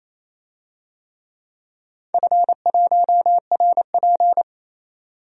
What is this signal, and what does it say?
Morse code, heard as a keyed tone, about 28 wpm, 700 Hz. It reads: F1RP
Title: DXCC-100 Call Signs - 4 Characters - 28wpm